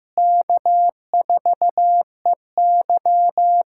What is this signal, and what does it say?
Morse code, heard as a keyed tone, about 15 wpm, 700 Hz. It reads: K4EY
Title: USA Callsign DE Callsign - 4 Characters - 15wpm